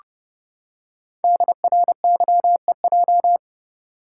Morse code, heard as a keyed tone, about 30 wpm, 700 Hz. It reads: DRYEJ
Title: Code Groups - 5 Letters - 30wpm